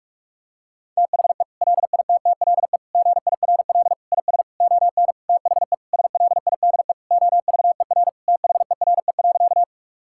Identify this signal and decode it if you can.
Morse code, heard as a keyed tone, about 41 wpm, 700 Hz. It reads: THE LITTLE GIRL IS ON THE SLIDE OVER THERE.
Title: Sentences from Top 1600 Words 45wpm